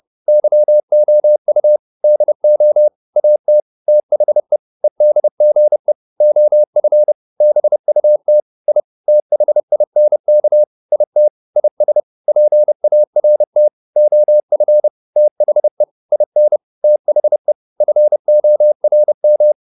Morse code, heard as a keyed tone, about 30 wpm, 600 Hz. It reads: YOU DO AT THE EDGE OF BUT I THINK IT IS PART OF THE IN THE FORM
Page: advanced-practice-files